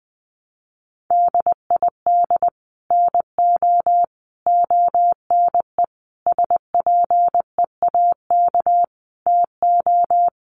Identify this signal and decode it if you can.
Morse code, heard as a keyed tone, about 20 wpm, 700 Hz. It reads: DID NO ONE SPEAK TO
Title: Sentences from Top 1100 Words 20wpm